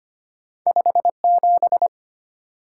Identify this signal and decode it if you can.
Morse code, heard as a keyed tone, about 25 wpm, 700 Hz. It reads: 57
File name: Two Letter-Number Combinations - 25wpm